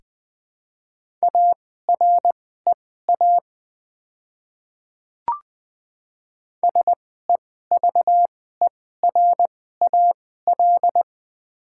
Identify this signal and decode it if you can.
Morse code, heard as a keyed tone, about 20 wpm, spaced as for 14 wpm, 690 Hz. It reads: AREA SEVERAL
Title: 2x Character Spacing - Top 500 Words - 20wpm